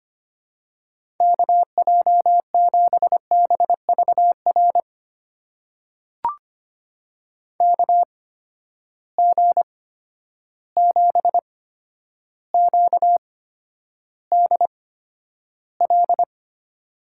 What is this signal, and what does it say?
Morse code, heard as a keyed tone, about 25 wpm, spaced as for 11 wpm, 700 Hz. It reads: KJ7BVR KG7QDL
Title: USA Call Signs - 6 Characters - 8x Character Spacing - 25wpm